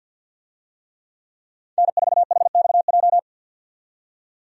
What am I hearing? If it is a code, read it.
Morse code, heard as a keyed tone, about 53 wpm, 700 Hz. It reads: N4HXJ
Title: USA Call Signs - 5 Characters - 50wpm